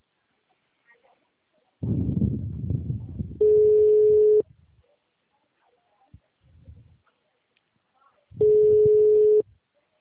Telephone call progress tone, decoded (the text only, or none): ringback tone